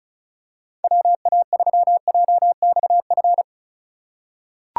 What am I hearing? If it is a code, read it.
Morse code, heard as a keyed tone, about 36 wpm, 700 Hz. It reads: WA3JXF
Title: USA Call Signs - 6 Characters - 2x Character Spacing - 35wpm